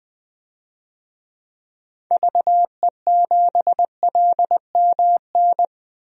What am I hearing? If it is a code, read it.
Morse code, heard as a keyed tone, about 20 wpm, 700 Hz. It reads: VE7LMN
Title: DXCC-100 Call Signs - 6 Characters - 20wpm